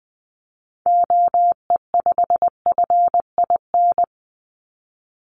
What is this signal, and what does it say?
Morse code, heard as a keyed tone, about 20 wpm, 700 Hz. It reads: OE5FIN
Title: Call Signs - 20wpm